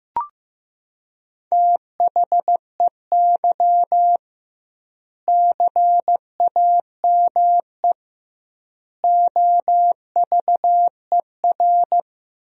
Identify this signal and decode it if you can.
Morse code, heard as a keyed tone, about 15 wpm, 700 Hz. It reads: THEY CAME OVER
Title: 2x Word-Spacing - Sentences from Top 300 Words - 15wpm